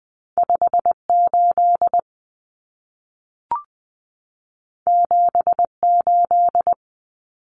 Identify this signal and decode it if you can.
Morse code, heard as a keyed tone, about 20 wpm, 700 Hz. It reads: 58 78
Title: Common Cut Numbers - 2-Digit 20wpm